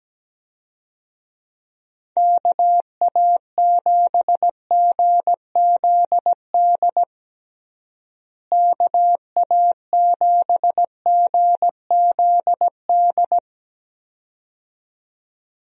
Morse code, heard as a keyed tone, about 17 wpm, 700 Hz. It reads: KA7GZD KA7GZD
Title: USA Call Signs - 6 Characters - Repeated 2x - 17wpm